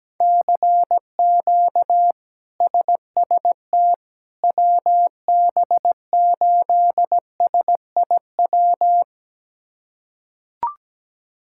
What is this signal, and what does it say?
Morse code, heard as a keyed tone, about 17 wpm, 700 Hz. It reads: CQ SST WB8SIW
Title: K1USN SST QSO Practice - 17wpm